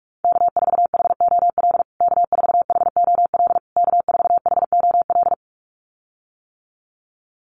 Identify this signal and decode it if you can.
Morse code, heard as a keyed tone, about 44 wpm, 700 Hz. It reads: K4HOL K4HOL K4HOL
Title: USA Call Signs Repeated 3x 45wpm